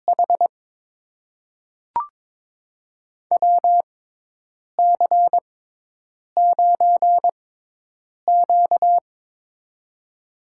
Morse code, heard as a keyed tone, about 23 wpm, spaced as for 8 wpm, 700 Hz. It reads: H WC9Q
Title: USA Call Signs - 4 Characters - 6x Character Spacing - 22wpm